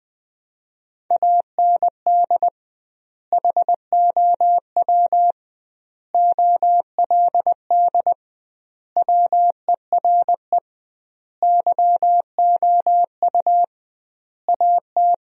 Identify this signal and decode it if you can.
Morse code, heard as a keyed tone, about 20 wpm, 700 Hz. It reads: AND HOW OLD WERE YOU AT
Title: 2x Word-Spacing - Sentences from Top 200 Words - 20wpm